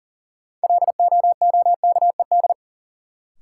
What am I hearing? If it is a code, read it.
Morse code, heard as a keyed tone, about 40 wpm, 700 Hz. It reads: LOOKED
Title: Sentences from Top 800 Words - New Word Drill - 40wpm